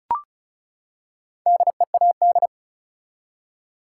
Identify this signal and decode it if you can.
Morse code, heard as a keyed tone, about 36 wpm, 700 Hz. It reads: DEAD
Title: Four Letter Words - 35wpm